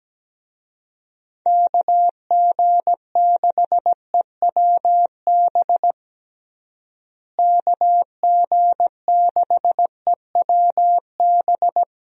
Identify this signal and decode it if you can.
Morse code, heard as a keyed tone, about 17 wpm, 700 Hz. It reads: KG6EWB KG6EWB
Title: USA Call Signs - 6 Characters - Repeated 3x - 17wpm